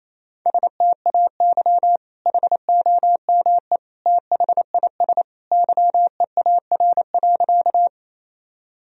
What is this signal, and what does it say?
Morse code, heard as a keyed tone, about 28 wpm, 700 Hz. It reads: STAY HOME THIS YEAR.
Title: Sentences from Top 1000 Words - Review Part 1 - 28wpm